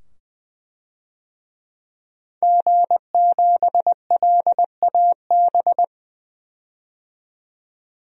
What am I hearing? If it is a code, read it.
Morse code, heard as a keyed tone, about 20 wpm, 700 Hz. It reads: G7LAB
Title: DXCC-100 Call Signs - 5 Characters - 20wpm